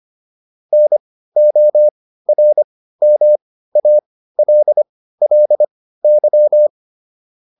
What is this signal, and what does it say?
Morse code, heard as a keyed tone, about 25 wpm, spaced as for 15 wpm, 600 Hz. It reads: NORMALLY